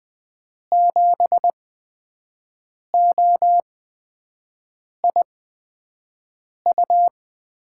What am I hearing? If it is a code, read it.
Morse code, heard as a keyed tone, about 20 wpm, spaced as for 5 wpm, 700 Hz. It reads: 7OIU
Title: USA Call Signs - 6 Characters - 8x Character Spacing - 20wpm